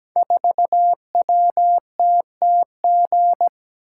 Morse code, heard as a keyed tone, about 17 wpm, 700 Hz. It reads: 4WTTG
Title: Code Groups - 5 Letter:Numbers - 17wpm